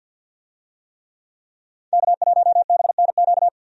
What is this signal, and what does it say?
Morse code, heard as a keyed tone, about 46 wpm, 700 Hz. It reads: K1BNQ
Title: USA Call Signs - 5 Characters - 50wpm